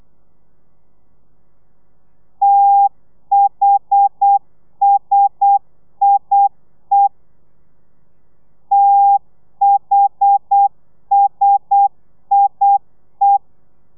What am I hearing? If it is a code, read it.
Morse code, heard as a keyed tone, about 8 wpm, 800 Hz. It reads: THSIE THSIE
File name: telegrafia / i2tfj / Lez1 35